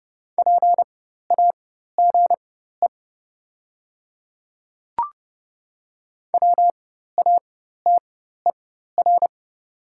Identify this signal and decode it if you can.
Morse code, heard as a keyed tone, about 30 wpm, spaced as for 14 wpm, 690 Hz. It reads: PAGE WATER
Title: 4x Character Spacing - Top 500 Words - 30wpm